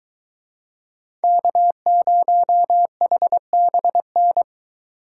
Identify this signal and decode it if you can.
Morse code, heard as a keyed tone, about 20 wpm, 700 Hz. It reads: K0HBN